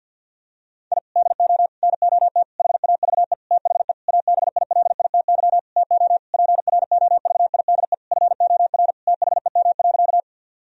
Morse code, heard as a keyed tone, about 50 wpm, 700 Hz. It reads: I DO NOT HAVE THE ABILITY TO PROVIDE FOR THEM.
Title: Sentences from Top 1000 Words - Review Part 3 - 50wpm